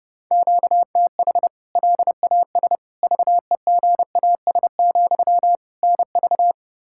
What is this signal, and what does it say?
Morse code, heard as a keyed tone, about 32 wpm, 700 Hz. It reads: QTH LAS VEGAS, NV
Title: K1USN SST QSO Practice 30wpm